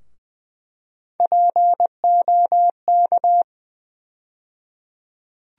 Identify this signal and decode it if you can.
Morse code, heard as a keyed tone, about 20 wpm, 700 Hz. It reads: POK
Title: USA Call Sign Postfixes - 3 Letters 20wpm